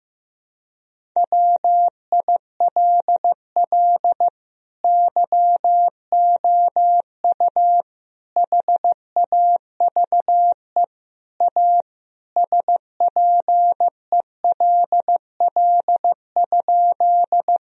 Morse code, heard as a keyed tone, about 15 wpm, 700 Hz. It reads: WILL YOU HAVE A SPELL?
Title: Sentences from Top 1000 Words - Review Part 1 - 15wpm